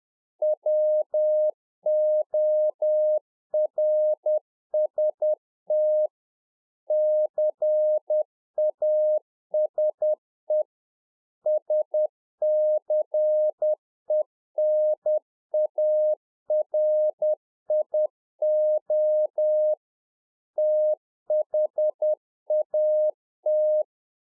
Morse code, heard as a keyed tone, about 10 wpm, 610 Hz. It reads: WORST CASE SCENARIO THAT